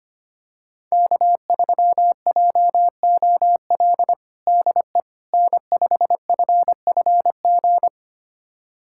Morse code, heard as a keyed tone, about 25 wpm, 700 Hz. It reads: K3JOL DE N5FFG
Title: USA Callsign DE Callsign - 5 Characters - 25wpm